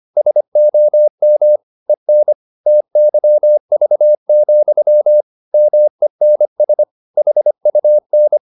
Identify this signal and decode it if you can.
Morse code, heard as a keyed tone, about 25 wpm, 600 Hz. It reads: SOM EN TYV, MENS HUN
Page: Ferie 25 wpm